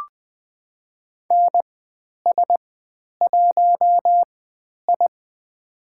Morse code, heard as a keyed tone, about 20 wpm, spaced as for 10 wpm, 700 Hz. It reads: NS1I
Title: 095 Lesson 16g - Letter c - Call Signs 20wpm